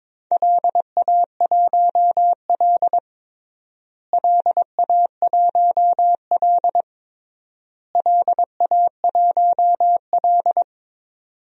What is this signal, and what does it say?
Morse code, heard as a keyed tone, about 22 wpm, 700 Hz. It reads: LA1L LA1L LA1L
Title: DXCC-100 Call Signs - 4 Characters - Repeated 3x - 22wpm